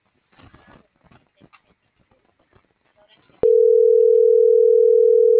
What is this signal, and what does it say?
Ringback tone (North America), 440 + 480 Hz.